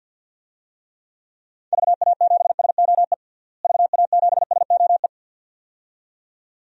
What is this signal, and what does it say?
Morse code, heard as a keyed tone, about 49 wpm, 700 Hz. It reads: VA7SOE VA7SOE
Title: DXCC-100 Call Signs - 6 Characters - Repeated 2x - 50wpm